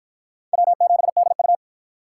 Morse code, heard as a keyed tone, about 53 wpm, 700 Hz. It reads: W7DU